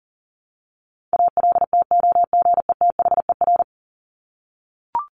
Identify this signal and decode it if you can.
Morse code, heard as a keyed tone, about 40 wpm, 700 Hz. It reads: ALTOGETHER
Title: Top 1501-1750 Words 40wpm